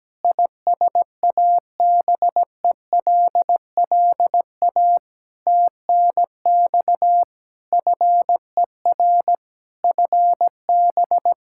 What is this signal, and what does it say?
Morse code, heard as a keyed tone, about 17 wpm, 700 Hz. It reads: ISABELLA TNX FER FB
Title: Practice QSO - Exchange 4 - Ending - 17wpm